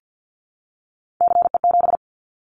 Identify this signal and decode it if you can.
Morse code, heard as a keyed tone, about 50 wpm, 700 Hz. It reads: CE7
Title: Callsign Prefixes for Top 100 DXCC - 50wpm